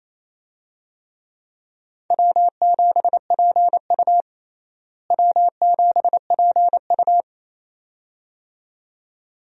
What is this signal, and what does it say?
Morse code, heard as a keyed tone, about 28 wpm, 700 Hz. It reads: W7PU W7PU
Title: USA Call Signs - 4 Characters - Repeated 2x - 28wpm